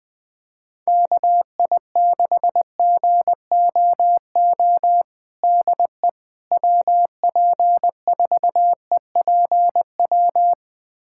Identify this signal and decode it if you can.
Morse code, heard as a keyed tone, about 20 wpm, 700 Hz. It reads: KI6GOO DE WP4EPW
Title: USA Callsign DE Callsign - 6 Characters - 20wpm